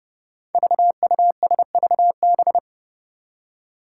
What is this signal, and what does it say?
Morse code, heard as a keyed tone, about 30 wpm, 700 Hz. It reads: VUSVB
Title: Five Challenging Letter-Number Combinations 30wpm